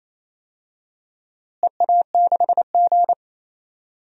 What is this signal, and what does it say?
Morse code, heard as a keyed tone, about 28 wpm, 700 Hz. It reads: EA6G
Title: DXCC-100 Call Signs - 4 Characters - Repeated 2x - 28wpm